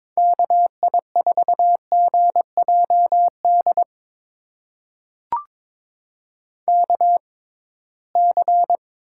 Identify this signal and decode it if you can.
Morse code, heard as a keyed tone, about 22 wpm, spaced as for 17 wpm, 700 Hz. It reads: KI4GJD KC
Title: USA Call Signs - 6 Characters - 6x Character Spacing - 22wpm